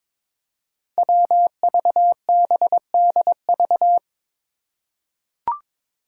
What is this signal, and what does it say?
Morse code, heard as a keyed tone, about 22 wpm, 700 Hz. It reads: WVBDV
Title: Five Challenging Letter-Number Combinations 22wpm